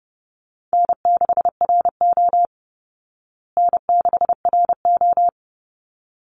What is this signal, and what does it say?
Morse code, heard as a keyed tone, about 30 wpm, 700 Hz. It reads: N6RO N6RO
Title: Easy Call Signs Repeated 2x - 30wpm